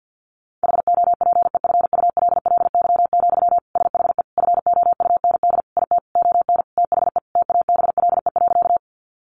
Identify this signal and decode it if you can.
Morse code, heard as a keyed tone, about 50 wpm, 700 Hz. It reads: HOPEFULLY, SHE FOUND IT ON THE TABLE.
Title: Sentences from Top 1500 Words 50wpm